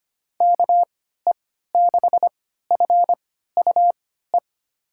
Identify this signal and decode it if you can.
Morse code, heard as a keyed tone, about 25 wpm, spaced as for 14 wpm, 700 Hz. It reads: KE6FUE
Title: USA Call Signs - 6 Characters - 3x Character Spacing - 25wpm